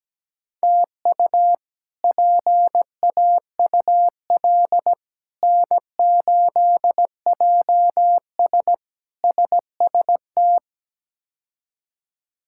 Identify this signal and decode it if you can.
Morse code, heard as a keyed tone, about 17 wpm, 700 Hz. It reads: TU PAUL N8JS SST
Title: K1USN SST QSO Practice - 17wpm